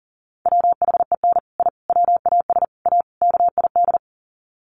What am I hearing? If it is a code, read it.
Morse code, heard as a keyed tone, about 40 wpm, 700 Hz. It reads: WHEN I WAS A KID
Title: Sets of 5 Words 40wpm